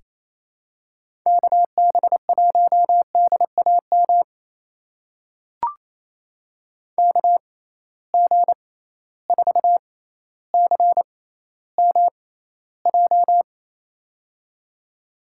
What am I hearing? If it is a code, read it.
Morse code, heard as a keyed tone, about 28 wpm, spaced as for 14 wpm, 700 Hz. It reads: KB1DAM KG4CMJ
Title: USA Call Signs - 6 Characters - 6x Character Spacing - 28wpm